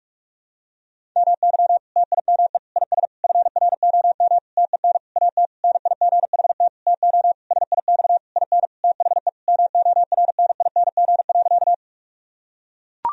Morse code, heard as a keyed tone, about 41 wpm, 700 Hz. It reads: MY TIME IS FROM TEN AT NIGHT TO SIX IN THE MORNING.
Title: Sentences from Top 1000 Words - Review Part 2 - 45wpm